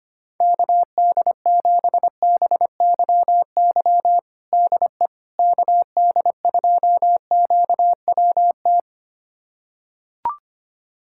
Morse code, heard as a keyed tone, about 25 wpm, 700 Hz. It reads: KD7BYY DE KD2QWT
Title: USA Callsign DE Callsign - 6 Characters - 25wpm